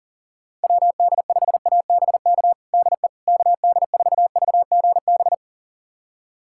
Morse code, heard as a keyed tone, about 40 wpm, 700 Hz. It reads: WD5ABK DE KD4VGB
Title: USA Callsign DE Callsign 40wpm